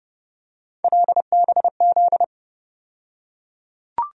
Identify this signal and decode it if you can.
Morse code, heard as a keyed tone, about 30 wpm, 700 Hz. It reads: LBZ
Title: USA Call Sign Postfixes - 3 Letters 30wpm